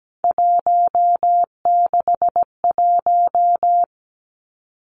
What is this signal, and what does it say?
Morse code, heard as a keyed tone, about 17 wpm, 700 Hz. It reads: 161
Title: Three Challenging Letter-Number Combinations 17wpm